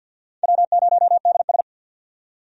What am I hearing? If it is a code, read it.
Morse code, heard as a keyed tone, about 47 wpm, 700 Hz. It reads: W0DS
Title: USA Call Signs - 4 Characters - 50wpm